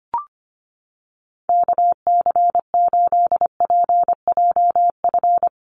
Morse code, heard as a keyed tone, about 25 wpm, 700 Hz. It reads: KC8PJF
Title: USA Call Signs Repeated 3x 25wpm